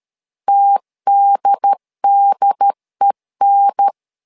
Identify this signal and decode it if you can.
Morse code, heard as a keyed tone, about 13 wpm, 800 Hz. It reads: TDDEN
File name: LEZIONI / 2 UHDN / 2 UHDN 12wpm 1456